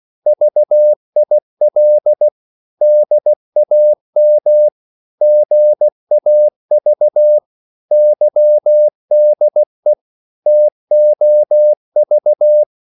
Morse code, heard as a keyed tone, about 16 wpm, 600 Hz. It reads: VIL DAM GAV YDE TOV
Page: Korte danske ord 16wpm